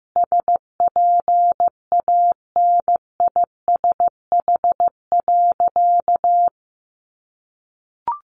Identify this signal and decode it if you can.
Morse code, heard as a keyed tone, about 15 wpm, 700 Hz. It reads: SPANISH.
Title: Additional Easy Phrases 15wpm